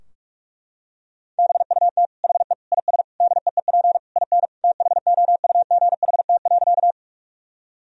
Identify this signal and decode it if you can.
Morse code, heard as a keyed tone, about 43 wpm, 700 Hz. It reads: BUT HE IS DEEP IN THOUGHT.
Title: Sentences from Top 1000 Words - Review Part 3 - 45wpm